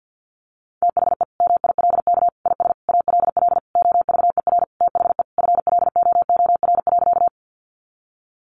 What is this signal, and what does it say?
Morse code, heard as a keyed tone, about 50 wpm, 700 Hz. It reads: THE MILK IS ALL OVER THE FLOOR.
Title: Additional Easy Phrases 50wpm